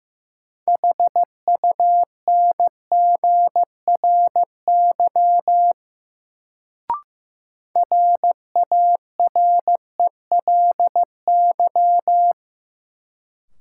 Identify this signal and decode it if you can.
Morse code, heard as a keyed tone, about 15 wpm, 700 Hz. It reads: HUNGRY RARELY
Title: Sentences from Top 1400 Words - New Word Drill - 15wpm